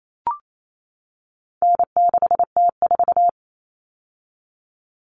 Morse code, heard as a keyed tone, about 28 wpm, 700 Hz. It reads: N6T4
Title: All Cut Numbers - 4-Digit 28wpm